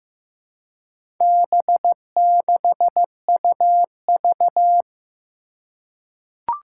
Morse code, heard as a keyed tone, about 15 wpm, 700 Hz. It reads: B6UV
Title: Four Challenging Letter-Number Combinations 15wpm